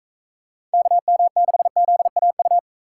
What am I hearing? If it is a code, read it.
Morse code, heard as a keyed tone, about 44 wpm, 700 Hz. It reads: KM6ZAU